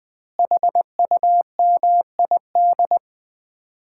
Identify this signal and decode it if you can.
Morse code, heard as a keyed tone, about 21 wpm, 700 Hz. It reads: HUMID
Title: Common QSO Phrases - 3x Word-Spacing - 20wpm